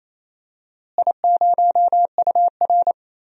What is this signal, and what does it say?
Morse code, heard as a keyed tone, about 28 wpm, 700 Hz. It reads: I0UR
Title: DXCC-100 Call Signs - 4 Characters - Repeated 2x - 28wpm